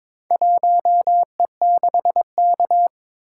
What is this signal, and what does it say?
Morse code, heard as a keyed tone, about 22 wpm, 700 Hz. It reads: 1E6K
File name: practice / 22wpm / Four Letter-Number Combinations - 22wpm